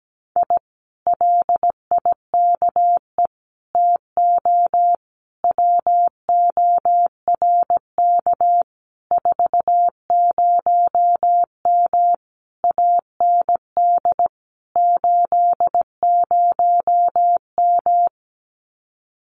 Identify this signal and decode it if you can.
Morse code, heard as a keyed tone, about 17 wpm, 700 Hz. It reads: I LIKE TO WORK 40M AND 80M
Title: Common QSO Phrases Encore 17wpm